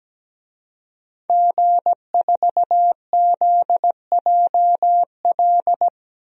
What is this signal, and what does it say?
Morse code, heard as a keyed tone, about 17 wpm, 700 Hz. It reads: G4ZJL
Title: DXCC-100 Call Signs - 5 Characters - 17wpm